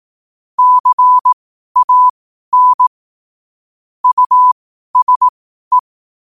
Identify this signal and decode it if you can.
Morse code, heard as a keyed tone, about 18 wpm, spaced as for 12 wpm, 1000 Hz. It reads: CAN USE
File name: qotdfiles / 12wpm